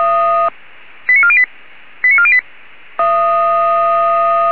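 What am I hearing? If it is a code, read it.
Five-tone selective call (ZVEI), heard as quick runs of five tones, tones of about 70 ms. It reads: 89408 89408
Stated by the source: ZVEI five-tone call received on an SDR.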